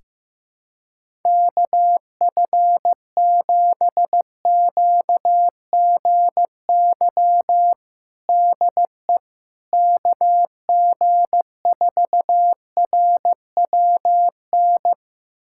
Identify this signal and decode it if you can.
Morse code, heard as a keyed tone, about 16 wpm, 700 Hz. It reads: KF7QGY DE KG4RWN
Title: USA Callsign DE Callsign - 6 Characters - 15wpm